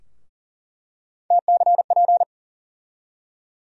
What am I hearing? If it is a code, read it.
Morse code, heard as a keyed tone, about 40 wpm, 700 Hz. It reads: TCP
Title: USA Call Sign Postfixes - 3 Letters 40wpm